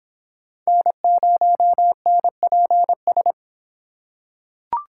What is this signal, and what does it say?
Morse code, heard as a keyed tone, about 26 wpm, 700 Hz. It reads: N0NPS